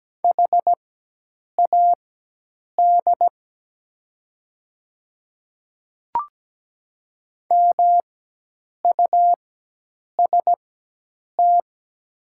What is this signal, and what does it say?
Morse code, heard as a keyed tone, about 17 wpm, spaced as for 8 wpm, 700 Hz. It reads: HAD MUST
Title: 4x Character Spacing - Top 500 Words - 17wpm